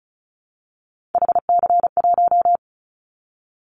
Morse code, heard as a keyed tone, about 34 wpm, 700 Hz. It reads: HC1
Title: Callsign Prefixes for Top 100 DXCC - 35wpm